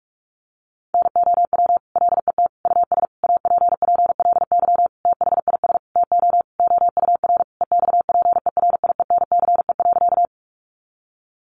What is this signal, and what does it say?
Morse code, heard as a keyed tone, about 44 wpm, 700 Hz. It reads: NOW LET US APPLY THIS TO OUR EXPERIENCE.
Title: Sentences from Top 1200 Words 45wpm